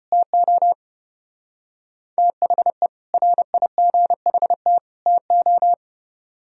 Morse code, heard as a keyed tone, about 29 wpm, 700 Hz. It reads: TO THE RIGHT TO
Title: Speed-Racing - Sets of 3 Words 25wpm